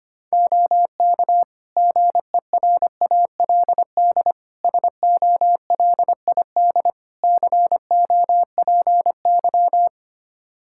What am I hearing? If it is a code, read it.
Morse code, heard as a keyed tone, about 25 wpm, 700 Hz. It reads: OK GERALD SOLID COPY
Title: Practice QSO - Exchange 3 - More Info - 25wpm